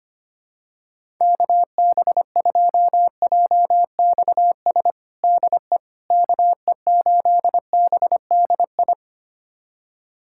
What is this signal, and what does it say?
Morse code, heard as a keyed tone, about 25 wpm, 700 Hz. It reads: KB2JXS DE KE8BDI
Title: USA Callsign DE Callsign 25wpm